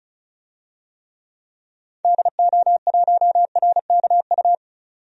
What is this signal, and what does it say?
Morse code, heard as a keyed tone, about 34 wpm, 700 Hz. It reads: DO1RKU
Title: DXCC-100 Call Signs - 6 Characters - 35wpm